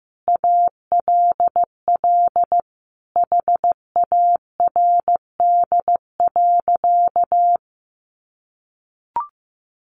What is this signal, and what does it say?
Morse code, heard as a keyed tone, about 15 wpm, 700 Hz. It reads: ALL HARD.
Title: Additional Easy Phrases 15wpm